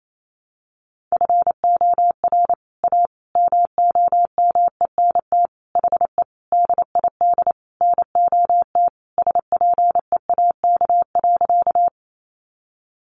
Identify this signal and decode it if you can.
Morse code, heard as a keyed tone, about 28 wpm, 700 Hz. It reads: FOR A MOMENT HE DID NOT SPEAK.
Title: Sentences from Top 1100 Words 28wpm